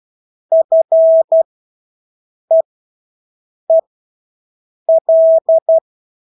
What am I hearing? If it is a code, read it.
Morse code, heard as a keyed tone, about 12 wpm, spaced as for 6 wpm, 650 Hz. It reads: FEEL
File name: Engelske ord 12-6 wpm